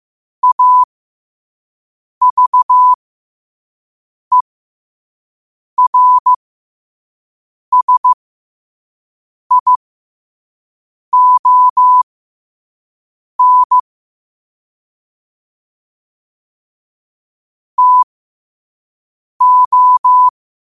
Morse code, heard as a keyed tone, about 15 wpm, spaced as for 5 wpm, 1000 Hz. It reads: AVERSION TO
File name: qotdfiles / 5wpm